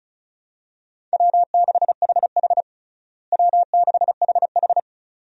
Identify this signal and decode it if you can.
Morse code, heard as a keyed tone, about 36 wpm, 700 Hz. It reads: W6HH W6HH
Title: USA Call Signs - 4 Characters - Repeated 2x - 35wpm